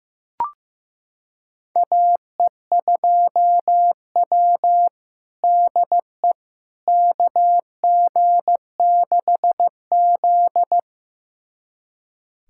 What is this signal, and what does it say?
Morse code, heard as a keyed tone, about 15 wpm, 700 Hz. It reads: AE2W DE KG6Z
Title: USA Callsign DE Callsign - 4 Characters - 15wpm